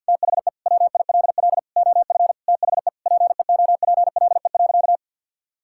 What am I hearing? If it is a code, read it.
Morse code, heard as a keyed tone, about 50 wpm, 700 Hz. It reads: THE WILL OF THE PEOPLE.
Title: Sentences from Top 1000 Words - Review Part 5 - 50wpm